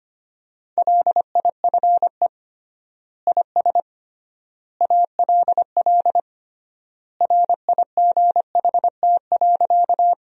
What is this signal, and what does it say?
Morse code, heard as a keyed tone, about 25 wpm, 700 Hz. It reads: LIFE IS ALL RIGHT.
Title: 3x Word-Spacing - Sentences from Top 300 Words - 25wpm